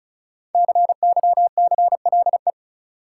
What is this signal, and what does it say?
Morse code, heard as a keyed tone, about 36 wpm, 700 Hz. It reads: CYCLE
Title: Sentences from Top 900 Words - New Word Drill - 35wpm